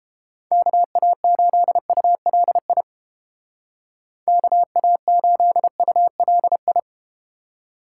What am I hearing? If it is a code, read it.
Morse code, heard as a keyed tone, about 32 wpm, 700 Hz. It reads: KA8ULI KA8ULI